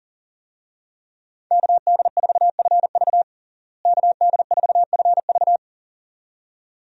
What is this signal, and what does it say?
Morse code, heard as a keyed tone, about 40 wpm, 700 Hz. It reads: KD4FV KD4FV
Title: USA Call Signs - 5 Characters - Repeated 2x - 40wpm